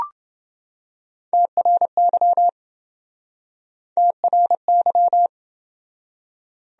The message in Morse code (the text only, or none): TRY TRY